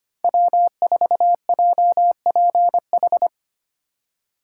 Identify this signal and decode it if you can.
Morse code, heard as a keyed tone, about 25 wpm, 700 Hz. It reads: W4JPH
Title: K1USN SST QSO Practice - 25wpm